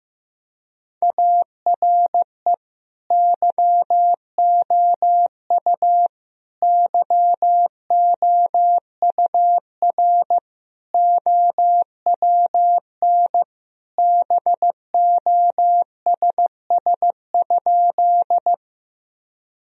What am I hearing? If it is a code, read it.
Morse code, heard as a keyed tone, about 15 wpm, 700 Hz. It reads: ARE YOU YOUR OWN BOSS?
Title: Sentences from Top 1000 Words - Review Part 4 - 15wpm